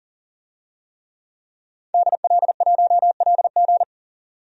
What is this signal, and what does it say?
Morse code, heard as a keyed tone, about 40 wpm, 700 Hz. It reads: DL1LG
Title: DXCC-100 Call Signs - 5 Characters - 40wpm